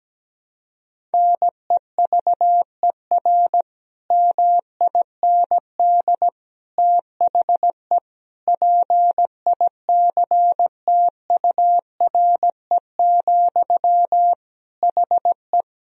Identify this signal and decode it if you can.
Morse code, heard as a keyed tone, about 17 wpm, 700 Hz. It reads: NEVER MIND THE PICTURE, HE
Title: Sentences from Top 1000 Words - Review Part 1 - 17wpm